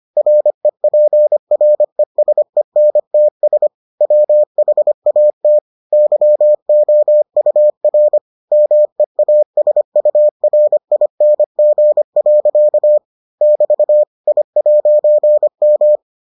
Morse code, heard as a keyed tone, about 25 wpm, 600 Hz. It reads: REPRESENTS WHAT YOUR MEASURING. <BT> I'M